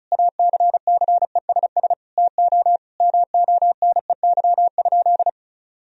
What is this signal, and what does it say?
Morse code, heard as a keyed tone, about 34 wpm, 700 Hz. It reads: ACCESS TO MONEY?
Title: Sentences from Top 1000 Words - Review Part 4 - 35wpm